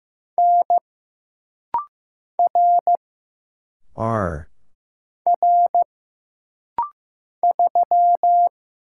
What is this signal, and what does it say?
Morse code, heard as a keyed tone, about 15 wpm, 700 Hz. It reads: N R R 3
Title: Single Letter-Number Alt - Rapid-Fire 15wpm